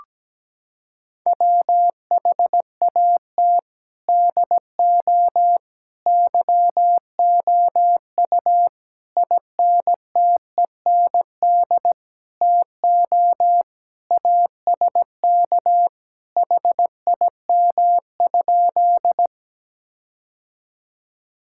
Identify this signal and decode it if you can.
Morse code, heard as a keyed tone, about 17 wpm, 700 Hz. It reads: WHAT DO YOU INTEND TO ASK HIM?
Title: Sentences from Top 1600 Words 17wpm